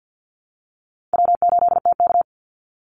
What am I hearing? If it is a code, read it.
Morse code, heard as a keyed tone, about 50 wpm, 700 Hz. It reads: W8TK
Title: Easy Call Signs Repeated 3x - 50wpm